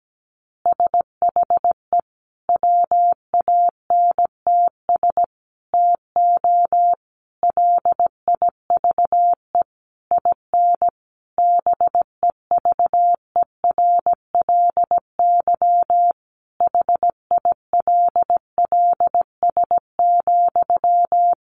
Morse code, heard as a keyed tone, about 17 wpm, 700 Hz. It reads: SHE WANTS TO LIVE IN BEVERLY HILLS,
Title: Easy Phrases 17wpm